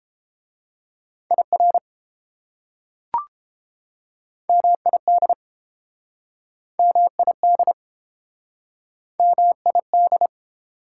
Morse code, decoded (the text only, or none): IR MID MID MID